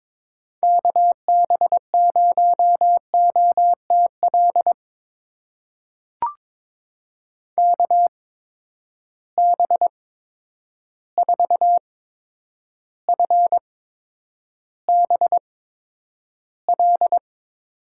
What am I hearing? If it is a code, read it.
Morse code, heard as a keyed tone, about 22 wpm, spaced as for 9 wpm, 700 Hz. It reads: KB0OTL KB4FBL
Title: USA Call Signs - 8x Character Spacing - 22wpm